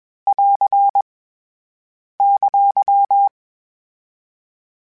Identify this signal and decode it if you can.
Morse code, heard as a keyed tone, about 21 wpm, 800 Hz. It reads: <AR> !